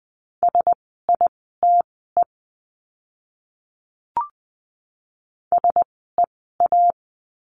Morse code, heard as a keyed tone, about 20 wpm, spaced as for 15 wpm, 700 Hz. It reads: SITE SEA
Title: 054 Lesson 10f - Letter r - Words 20wpm